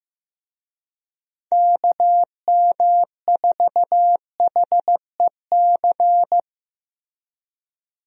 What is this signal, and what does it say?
Morse code, heard as a keyed tone, about 15 wpm, 700 Hz. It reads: KM4HEC
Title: USA Call Signs - 6 Characters - Repeated 2x - 15wpm